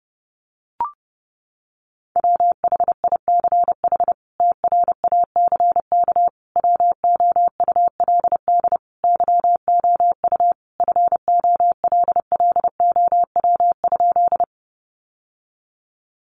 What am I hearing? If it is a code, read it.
Morse code, heard as a keyed tone, about 30 wpm, 700 Hz. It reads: WHICH TRACK WOULD YOU FOLLOW?
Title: Sentences from Top 1100 Words 30wpm